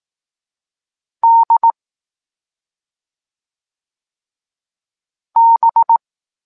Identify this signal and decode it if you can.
Morse code, heard as a keyed tone, about 18 wpm, 920 Hz. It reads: D B
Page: Tekst 14 – doskonalenie odbioru poznanych liter